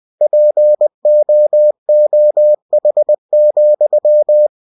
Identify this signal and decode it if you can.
Morse code, heard as a keyed tone, about 20 wpm, 600 Hz. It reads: POOH,
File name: cw-books / winnie-the-pooh / winnie-20wpm-page-0030